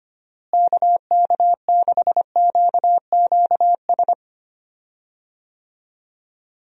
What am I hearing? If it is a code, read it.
Morse code, heard as a keyed tone, about 25 wpm, 700 Hz. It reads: KK6QQS
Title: USA Call Signs - 6 Characters - Repeated 2x - 25wpm